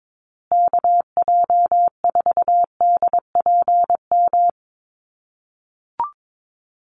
Morse code, heard as a keyed tone, about 22 wpm, 700 Hz. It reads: KJ4DPM
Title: USA Call Signs Repeated 2x 22wpm